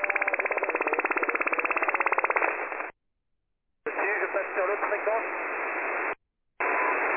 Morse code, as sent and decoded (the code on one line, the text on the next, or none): none